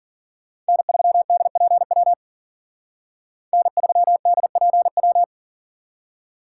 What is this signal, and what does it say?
Morse code, heard as a keyed tone, about 52 wpm, 700 Hz. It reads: N3DPW N3DPW